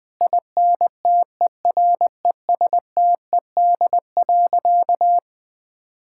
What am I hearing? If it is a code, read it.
Morse code, heard as a keyed tone, about 20 wpm, 700 Hz. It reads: INTERESTED.
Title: Sentences from Top 1000 Words - Review Part 6 - 20wpm